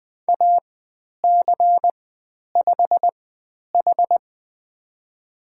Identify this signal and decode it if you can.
Morse code, heard as a keyed tone, about 20 wpm, spaced as for 10 wpm, 700 Hz. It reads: AC5H
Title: 102 Lesson 17g - Letter u - Call Signs 20wpm